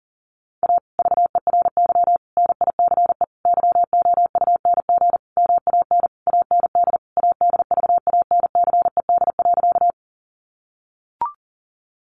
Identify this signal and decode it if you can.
Morse code, heard as a keyed tone, about 38 wpm, 700 Hz. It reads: A VERY NICE YOUNG MAN AND ADVANCED.
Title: Sentences from Top 1200 Words 40wpm